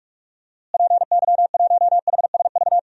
Morse code, heard as a keyed tone, about 44 wpm, 700 Hz. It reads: PY1HSV
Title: DXCC-100 Call Signs - 6 Characters - Repeated 2x - 45wpm